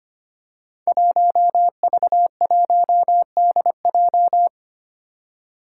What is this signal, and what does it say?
Morse code, heard as a keyed tone, about 25 wpm, 700 Hz. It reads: 1V1DJ
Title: Five Challenging Letter-Number Combinations 25wpm